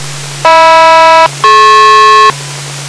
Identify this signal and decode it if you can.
Aircraft SELCAL, heard as two pairs of tones. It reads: HM-DP